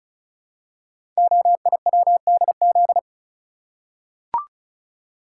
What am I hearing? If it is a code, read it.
Morse code, heard as a keyed tone, about 36 wpm, 700 Hz. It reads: OIWDZ
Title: Code Groups - 5 Letters - 35wpm